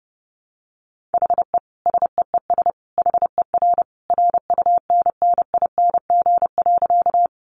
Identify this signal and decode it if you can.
Morse code, heard as a keyed tone, about 30 wpm, 700 Hz. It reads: HE SEES HER RUNNING.
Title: Additional Easy Phrases 30wpm